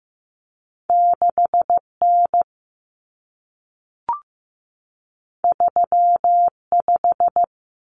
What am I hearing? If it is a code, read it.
Morse code, heard as a keyed tone, about 15 wpm, 700 Hz. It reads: 6N 35
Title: Common Cut Numbers - 2-Digit 15wpm